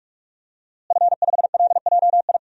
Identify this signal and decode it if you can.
Morse code, heard as a keyed tone, about 46 wpm, 700 Hz. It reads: F5LJI
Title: DXCC-100 Call Signs - 5 Characters - Repeated 3x - 45wpm